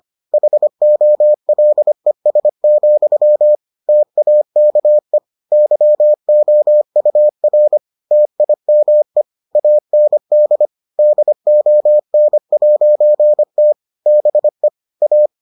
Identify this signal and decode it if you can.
Morse code, heard as a keyed tone, about 25 wpm, 600 Hz. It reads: HOLES, TAKE YOUR TIME AND DON'T BE A